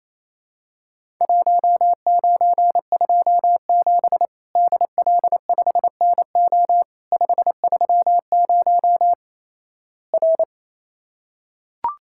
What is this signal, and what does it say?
Morse code, heard as a keyed tone, about 28 wpm, 700 Hz. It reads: 1927 DL5NO 530 R
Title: Worked All Europe DX Contest - QTC Exchange - 28wpm